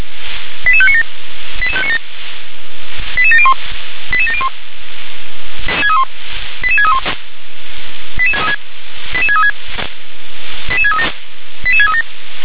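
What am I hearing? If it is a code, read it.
Five-tone selective call (ZVEI), heard as quick runs of five tones, tones of about 80 ms. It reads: 88597 88597 88701 88701 88631 88636 88636 88637 88637
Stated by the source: ZVEI five-tone call received via SDR.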